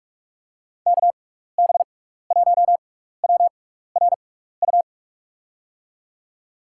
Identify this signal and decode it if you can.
Morse code, heard as a keyed tone, about 41 wpm, spaced as for 15 wpm, 700 Hz. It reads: KB1WRU
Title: USA Call Signs - 6 Characters - 6x Character Spacing - 45wpm